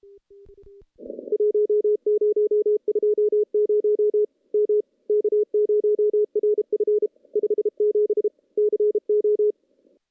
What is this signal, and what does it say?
Morse code, heard as a keyed tone, about 32 wpm, 420 Hz. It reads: 1020 M K0RF 57 CO
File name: contests / CW-SS-k5tr- -1300